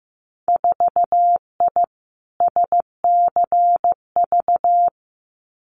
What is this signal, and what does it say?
Morse code, heard as a keyed tone, about 15 wpm, 700 Hz. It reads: 4I SCV
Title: Winter Field Day Exchanges Encore 15wpm